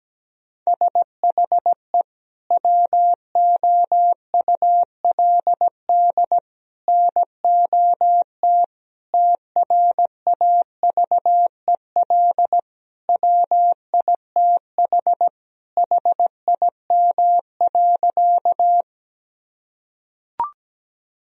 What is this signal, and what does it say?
Morse code, heard as a keyed tone, about 17 wpm, 700 Hz. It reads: SHE WOULD NOT TRAVEL WITH HIM.
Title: Sentences from Top 1000 Words - Review Part 2 - 17wpm